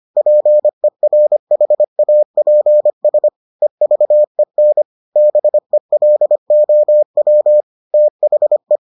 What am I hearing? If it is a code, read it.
Morse code, heard as a keyed tone, about 25 wpm, 600 Hz. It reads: PERHAPS EVEN BELOW THE